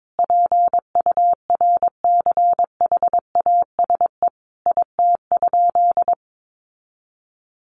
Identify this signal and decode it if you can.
Morse code, heard as a keyed tone, about 22 wpm, 700 Hz. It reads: PURCHASE IT?
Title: Sentences from Top 800 Words 22wpm